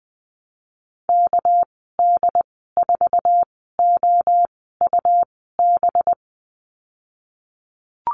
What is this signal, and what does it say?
Morse code, heard as a keyed tone, about 20 wpm, spaced as for 15 wpm, 700 Hz. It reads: KD4OUB
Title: 213 Lesson 34g - Number 8 - Call Signs 20wpm